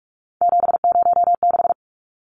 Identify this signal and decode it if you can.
Morse code, heard as a keyed tone, about 44 wpm, 700 Hz. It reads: 706
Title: Numbers - 3-Digit 45wpm